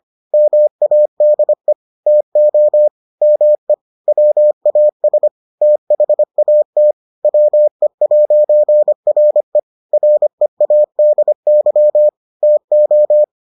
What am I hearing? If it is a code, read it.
Morse code, heard as a keyed tone, about 25 wpm, 600 Hz. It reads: MADE TO ME WAS THAT WE'RE READY TO